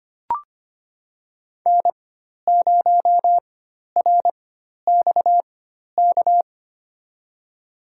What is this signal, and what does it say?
Morse code, heard as a keyed tone, about 25 wpm, spaced as for 12 wpm, 700 Hz. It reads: N0RXK
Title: USA Call Signs - 5 Characters - 4x Character Spacing - 25wpm